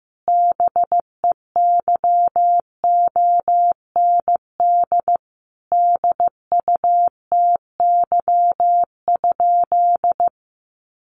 Morse code, heard as a keyed tone, about 15 wpm, 700 Hz. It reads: BEYOND DUTY?
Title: Sentences from Top 1500 Words 15wpm